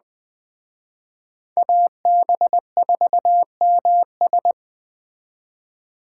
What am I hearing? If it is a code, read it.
Morse code, heard as a keyed tone, about 20 wpm, 700 Hz. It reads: AB4MS
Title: Speed Racing - USA 2x2 Call Signs 17wpm